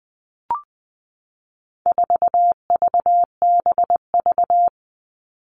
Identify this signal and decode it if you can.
Morse code, heard as a keyed tone, about 20 wpm, 700 Hz. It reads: 4VBV
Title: All Cut Numbers - 4-Digit 20wpm